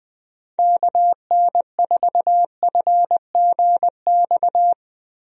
Morse code, heard as a keyed tone, about 20 wpm, 700 Hz. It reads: KN4FGX
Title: USA Call Signs - 6 Characters - Repeated 2x - 20wpm